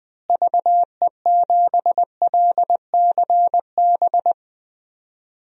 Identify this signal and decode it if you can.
Morse code, heard as a keyed tone, about 20 wpm, 700 Hz. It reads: VE7LCB
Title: DXCC-100 Call Signs - 6 Characters - Repeated 3x - 20wpm